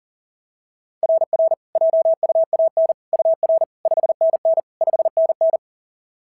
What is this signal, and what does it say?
Morse code, heard as a keyed tone, about 40 wpm, 690 Hz. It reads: RR JUAN UR 5NN 5NN
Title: SOTA QSO Practice 40wpm